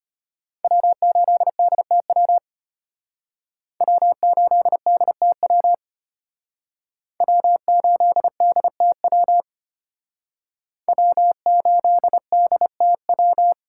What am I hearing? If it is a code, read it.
Morse code, heard as a keyed tone, about 32 wpm, 700 Hz. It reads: W8DTW W8DTW W8DTW W8DTW